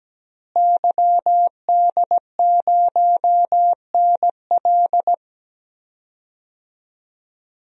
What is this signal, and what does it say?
Morse code, heard as a keyed tone, about 17 wpm, 700 Hz. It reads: YD0NL
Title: DXCC-100 Call Signs - 5 Characters - Repeated 2x - 17wpm